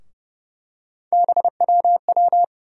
Morse code, heard as a keyed tone, about 31 wpm, 700 Hz. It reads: BWW
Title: USA Call Sign Postfixes - 3 Letters 30wpm